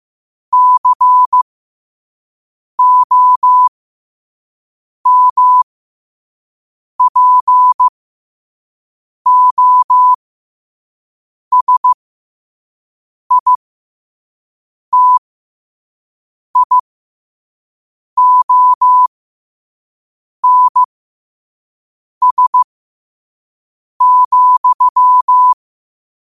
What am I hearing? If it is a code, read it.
Morse code, heard as a keyed tone, about 15 wpm, spaced as for 5 wpm, 1000 Hz. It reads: COMPOSITIONS,
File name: qotdfiles / 5wpm